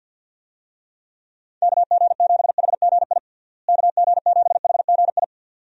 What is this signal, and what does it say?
Morse code, heard as a keyed tone, about 52 wpm, 700 Hz. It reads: KG7HGI KG7HGI
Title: USA Call Signs - 6 Characters - Repeated 2x - 50wpm